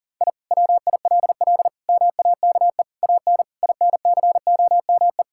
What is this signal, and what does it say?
Morse code, heard as a keyed tone, about 37 wpm, 700 Hz. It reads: I WILL MAKE AN INCOME
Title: Sentences from Top 1000 Words - Review Part 5 - 40wpm